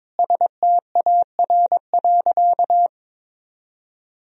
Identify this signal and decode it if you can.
Morse code, heard as a keyed tone, about 22 wpm, 700 Hz. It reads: STAR.
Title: Sentences from Top 1000 Words - Review Part 2 - 22wpm